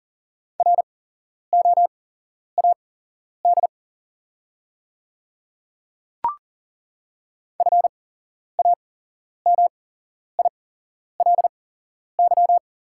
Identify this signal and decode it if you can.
Morse code, heard as a keyed tone, about 40 wpm, spaced as for 11 wpm, 700 Hz. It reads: ROAD FAMILY
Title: 8x Character Spacing - Top 500 Words - 40wpm